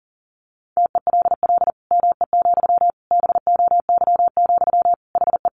Morse code, heard as a keyed tone, about 40 wpm, 700 Hz. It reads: TELL ME, BOY, HE
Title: Sentences from Top 1500 Words 40wpm